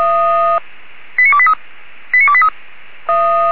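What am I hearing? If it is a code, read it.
Five-tone selective call (ZVEI), heard as quick runs of five tones, tones of about 80 ms. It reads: 89283 89283
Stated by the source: ZVEI five-tone call 89284 received by SDR on channel 464.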